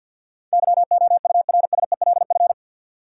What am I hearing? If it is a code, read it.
Morse code, heard as a keyed tone, about 53 wpm, 700 Hz. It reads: YOURSELF
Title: Sentences from Top 600 Words - New Word Drill - 50wpm